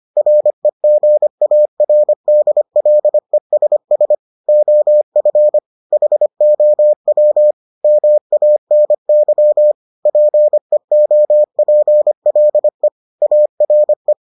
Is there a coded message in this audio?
Yes: Morse code from 0.2 to 14.1 s, heard as a keyed tone, about 25 wpm, 600 Hz. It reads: REGARDLESS OF HOW MANY PEOPLE ARE